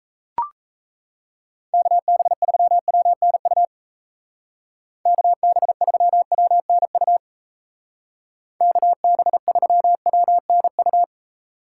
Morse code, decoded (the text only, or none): KB3WNU KB3WNU KB3WNU